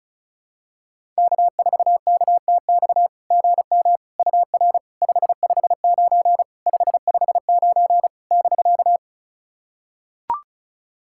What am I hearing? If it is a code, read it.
Morse code, heard as a keyed tone, about 36 wpm, 690 Hz. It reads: K4KTX GM UR 559 559 <BK>
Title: SOTA QSO Practice 35wpm